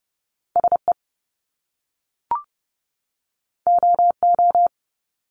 Morse code, heard as a keyed tone, about 30 wpm, 700 Hz. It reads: S E O O
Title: Top Two Letter Combinations in Words 30wpm